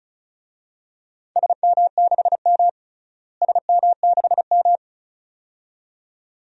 Morse code, heard as a keyed tone, about 36 wpm, 700 Hz. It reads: SM6M SM6M
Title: DXCC-100 Call Signs - 4 Characters - Repeated 2x - 35wpm